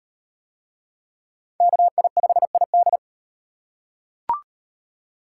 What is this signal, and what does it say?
Morse code, heard as a keyed tone, about 38 wpm, 700 Hz. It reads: KI5ID